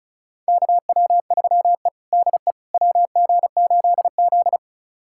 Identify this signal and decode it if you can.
Morse code, heard as a keyed tone, about 34 wpm, 700 Hz. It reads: KW3E DE WG8Z
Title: USA Callsign DE Callsign - 4 Characters - 35wpm